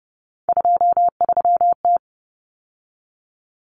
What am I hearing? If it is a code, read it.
Morse code, heard as a keyed tone, about 30 wpm, 700 Hz. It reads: 23T
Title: Common Cut Numbers - 3-Digit 30wpm